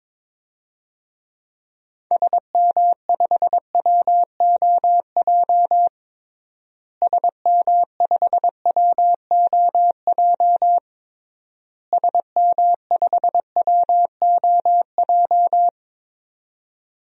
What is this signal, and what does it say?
Morse code, heard as a keyed tone, about 22 wpm, 700 Hz. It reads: SM5WOJ SM5WOJ SM5WOJ
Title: DXCC-100 Call Signs - 6 Characters - Repeated 3x - 22wpm